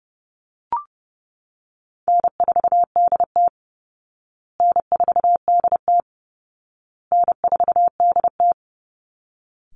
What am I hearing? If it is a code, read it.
Morse code, heard as a keyed tone, about 30 wpm, 700 Hz. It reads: N4DT N4DT N4DT
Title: Call Signs Repeated 2x - 30wpm